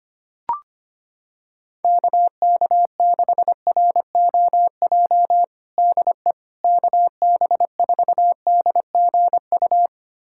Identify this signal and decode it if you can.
Morse code, heard as a keyed tone, about 25 wpm, 700 Hz. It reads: KK6ROJ DE KB4DGU
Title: USA Callsign DE Callsign - 6 Characters - 25wpm